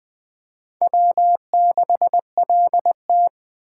Morse code, heard as a keyed tone, about 20 wpm, 700 Hz. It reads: W6LT